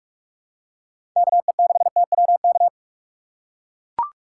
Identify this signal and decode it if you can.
Morse code, heard as a keyed tone, about 41 wpm, 700 Hz. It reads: KE6TWK
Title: USA Call Signs - 6 Characters - 2x Character Spacing - 45wpm